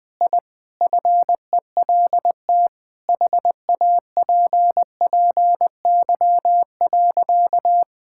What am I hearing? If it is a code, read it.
Morse code, heard as a keyed tone, about 20 wpm, 700 Hz. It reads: I FELT HAPPY.
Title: Sentences with 5-Word Phrases - 20wpm